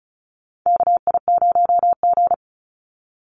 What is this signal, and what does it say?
Morse code, heard as a keyed tone, about 34 wpm, 700 Hz. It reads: KI0G
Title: NA SOTA Chaser Callsigns 35wpm